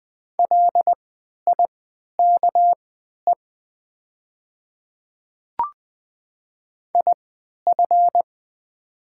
Morse code, heard as a keyed tone, about 20 wpm, spaced as for 11 wpm, 700 Hz. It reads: LIKE IF
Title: 3x Character Spacing - Top 100 Words - 20wpm